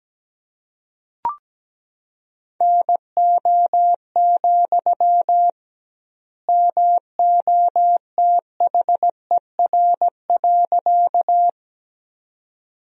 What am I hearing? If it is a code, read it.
Morse code, heard as a keyed tone, about 17 wpm, 700 Hz. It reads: NO, MOTHER.
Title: 2x Word-Spacing - Sentences from Top 200 Words - 17wpm